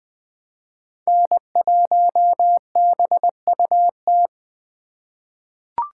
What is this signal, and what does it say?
Morse code, heard as a keyed tone, about 20 wpm, 700 Hz. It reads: N1BUT
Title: USA Call Signs - 5 Characters - 3x Character Spacing - 20wpm